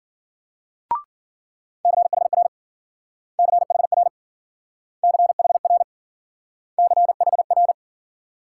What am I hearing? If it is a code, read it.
Morse code, heard as a keyed tone, about 50 wpm, 700 Hz. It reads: CHR CHR CHR CHR
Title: Speed-Racing Multi-letter Phonemes - 40wpm